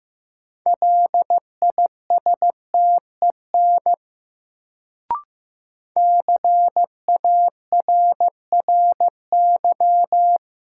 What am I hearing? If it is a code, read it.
Morse code, heard as a keyed tone, about 15 wpm, 700 Hz. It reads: LISTEN CARRY
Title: Sentences from Top 400 Words - New Word Drill - 15wpm